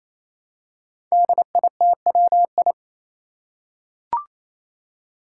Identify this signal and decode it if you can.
Morse code, heard as a keyed tone, about 28 wpm, 700 Hz. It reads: DITWI
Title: Five Challenging Letter-Number Combinations 28wpm